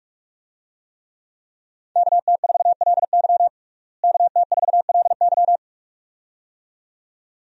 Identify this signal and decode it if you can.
Morse code, heard as a keyed tone, about 46 wpm, 700 Hz. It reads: KT4LY KT4LY
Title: USA Call Signs - 5 Characters - Repeated 2x - 45wpm